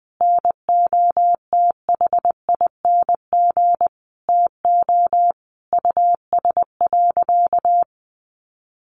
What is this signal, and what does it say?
Morse code, heard as a keyed tone, about 20 wpm, 700 Hz. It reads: NOTHING TO US.
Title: Sentences from Top 500 Words 20wpm